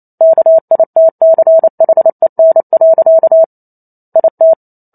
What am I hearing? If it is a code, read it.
Morse code, heard as a keyed tone, about 28 wpm, 650 Hz. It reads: KITCHEN. IT